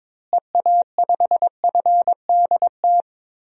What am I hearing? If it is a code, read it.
Morse code, heard as a keyed tone, about 22 wpm, 700 Hz. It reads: EA5FDT
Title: DXCC-100 Call Signs - 6 Characters - Repeated 2x - 22wpm